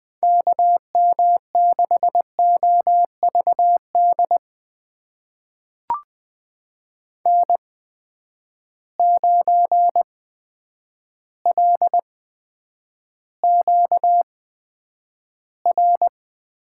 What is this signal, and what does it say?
Morse code, heard as a keyed tone, about 20 wpm, spaced as for 9 wpm, 700 Hz. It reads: KM6OVD N9LQR
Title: USA Call Signs - 8x Character Spacing - 20wpm